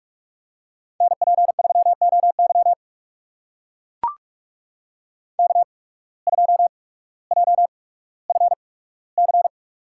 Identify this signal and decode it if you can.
Morse code, heard as a keyed tone, about 42 wpm, spaced as for 19 wpm, 700 Hz. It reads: NP3OY K2JFC
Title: USA Call Signs - 5 Characters - 8x Character Spacing - 45wpm